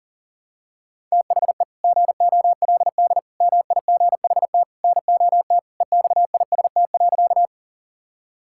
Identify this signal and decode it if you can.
Morse code, heard as a keyed tone, about 40 wpm, 690 Hz. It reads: THE GOLD MIGHT NOT EXIST.
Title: Sentences from Top 1000 Words - Review Part 5 - 40wpm